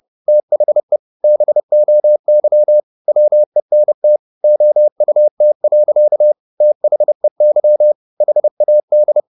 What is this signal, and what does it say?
Morse code, heard as a keyed tone, about 30 wpm, 600 Hz. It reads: THE BOY WENT OUT. THEY HAD
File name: the old man and the sea / the old man and the sea 1 30wpm